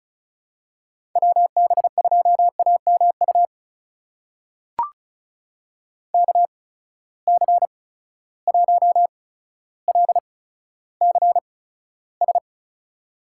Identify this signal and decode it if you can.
Morse code, heard as a keyed tone, about 36 wpm, spaced as for 15 wpm, 700 Hz. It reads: WB2AMU KC1LCS
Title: USA Call Signs - 6 Characters - 8x Character Spacing - 35wpm